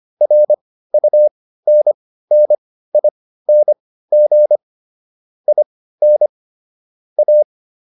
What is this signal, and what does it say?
Morse code, heard as a keyed tone, about 25 wpm, spaced as for 15 wpm, 600 Hz. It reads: RUNNING IN A